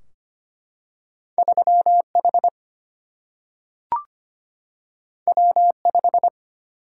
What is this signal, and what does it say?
Morse code, heard as a keyed tone, about 25 wpm, 700 Hz. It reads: 3H W5
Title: Single-Letter Single-Number 25wpm